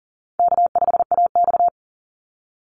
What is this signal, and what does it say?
Morse code, heard as a keyed tone, about 38 wpm, 700 Hz. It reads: K5AX
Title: Easy Call Signs - 40wpm